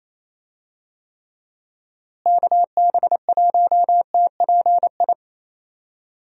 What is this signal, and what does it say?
Morse code, heard as a keyed tone, about 28 wpm, 700 Hz. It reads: KB1TPI
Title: USA Call Signs - 6 Characters - Repeated 3x - 28wpm